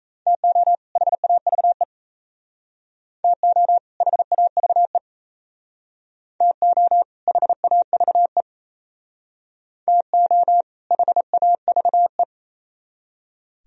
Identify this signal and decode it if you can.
Morse code, heard as a keyed tone, about 36 wpm, 700 Hz. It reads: TO HAVE TO HAVE TO HAVE TO HAVE